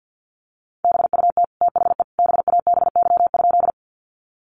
Morse code, heard as a keyed tone, about 50 wpm, 700 Hz. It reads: BUT THE BABY?
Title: Sentences from Top 1300 Words 50wpm